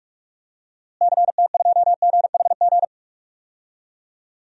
Morse code, heard as a keyed tone, about 46 wpm, 700 Hz. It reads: CT2GHG
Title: DXCC-100 Call Signs - 6 Characters - Repeated 3x - 45wpm